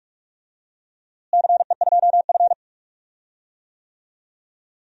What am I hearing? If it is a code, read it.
Morse code, heard as a keyed tone, about 46 wpm, 700 Hz. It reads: CE2F
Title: DXCC-100 Call Signs - 4 Characters - Repeated 3x - 45wpm